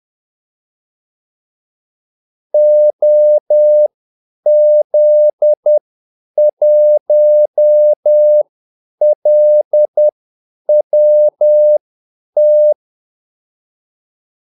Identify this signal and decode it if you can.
Morse code, heard as a keyed tone, about 10 wpm, spaced as for 8 wpm, 600 Hz. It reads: OZ1LWT
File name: Callsigns-10-8wpm